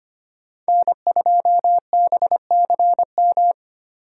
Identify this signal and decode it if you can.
Morse code, heard as a keyed tone, about 30 wpm, 700 Hz. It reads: N2BCM
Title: Speed Racing - USA 1x3 Call Signs 25wpm